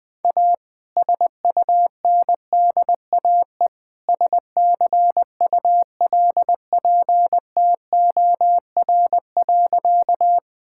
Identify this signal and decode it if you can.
Morse code, heard as a keyed tone, about 21 wpm, 700 Hz. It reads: A SUNDAE SCULPTOR.
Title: All-Age Appropriate Jokes - 20wpm